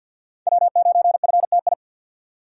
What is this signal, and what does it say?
Morse code, heard as a keyed tone, about 49 wpm, 700 Hz. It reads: W9FTI
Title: USA Call Signs - 2x Character Spacing - 50wpm